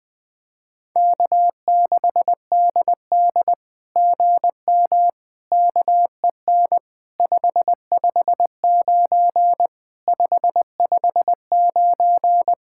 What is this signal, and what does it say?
Morse code, heard as a keyed tone, about 18 wpm, 700 Hz. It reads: K6DD GM KEN 559 559
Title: SOTA QSO Practice 20wpm